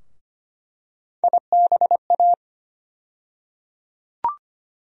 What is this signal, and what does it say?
Morse code, heard as a keyed tone, about 25 wpm, 700 Hz. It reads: IBA
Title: USA Call Sign Postfixes - 3 Letters 25wpm